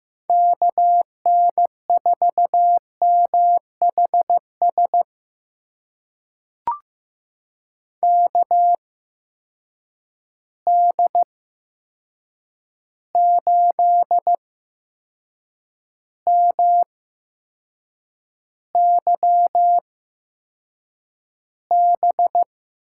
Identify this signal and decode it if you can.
Morse code, heard as a keyed tone, about 15 wpm, spaced as for 6 wpm, 700 Hz. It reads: KN4MHS KD8MYB
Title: USA Call Signs - 6 Characters - 8x Character Spacing - 15wpm